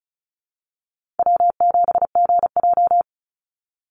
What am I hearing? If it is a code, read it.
Morse code, heard as a keyed tone, about 34 wpm, 700 Hz. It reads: W7GJ
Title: NA SOTA Chaser Callsigns 35wpm